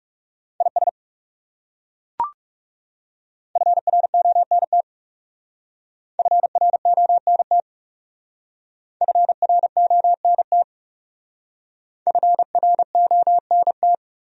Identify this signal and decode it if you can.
Morse code, heard as a keyed tone, about 37 wpm, 700 Hz. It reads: IS FRONT FRONT FRONT FRONT
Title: Speed-Racing Top 500 Words 30wpm